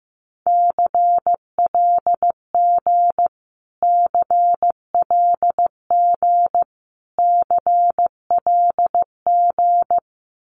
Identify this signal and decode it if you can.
Morse code, heard as a keyed tone, about 15 wpm, 700 Hz. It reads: CLG CLG CLG
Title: Abbreviations Repeated 3x - 15wpm